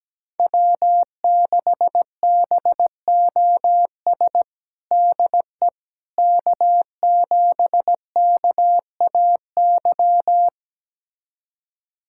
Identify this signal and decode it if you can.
Morse code, heard as a keyed tone, about 17 wpm, 700 Hz. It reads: W6BOS DE K7KAY
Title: USA Callsign DE Callsign - 5 Characters - 17wpm